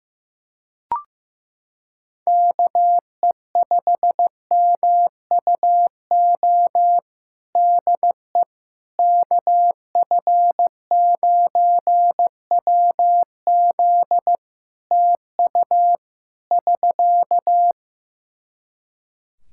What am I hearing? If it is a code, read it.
Morse code, heard as a keyed tone, about 15 wpm, 700 Hz. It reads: KE5MUO DE KF9WZ TU <SK>
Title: K1USN SST QSO Practice 15wpm